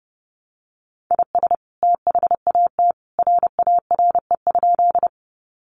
Morse code, heard as a keyed tone, about 30 wpm, 700 Hz. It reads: IS THAT RARE?
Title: Sentences from Top 1100 Words 30wpm